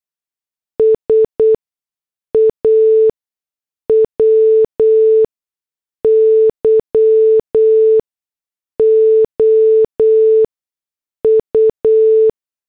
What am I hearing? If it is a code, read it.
Morse code, heard as a keyed tone, about 8 wpm, spaced as for 6 wpm, 440 Hz. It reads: SAW YOU
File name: morse / headlines